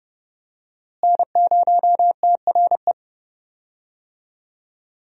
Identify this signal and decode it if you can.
Morse code, heard as a keyed tone, about 30 wpm, 700 Hz. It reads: N0TRE
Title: USA Call Signs - 5 Characters - Repeated 2x - 30wpm